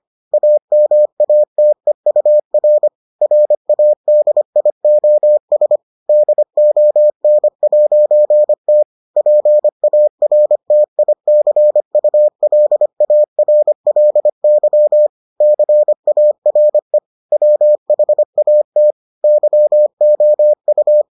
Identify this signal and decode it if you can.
Morse code, heard as a keyed tone, about 25 wpm, 600 Hz. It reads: AMATEUR RADIOS DON'T PARTICULARLY CARE WHAT YOU